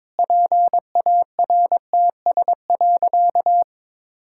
Morse code, heard as a keyed tone, about 21 wpm, 700 Hz. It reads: PARTS.
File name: Sentences with 7-Word Phrases - 22wpm